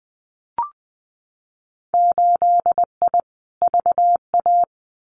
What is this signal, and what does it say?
Morse code, heard as a keyed tone, about 20 wpm, 700 Hz. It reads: 8I VA
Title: Winter Field Day Exchanges Encore 20wpm